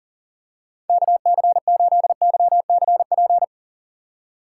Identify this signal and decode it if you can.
Morse code, heard as a keyed tone, about 37 wpm, 700 Hz. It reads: KC8YCP
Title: USA Call Signs - 6 Characters - 3x Character Spacing - 40wpm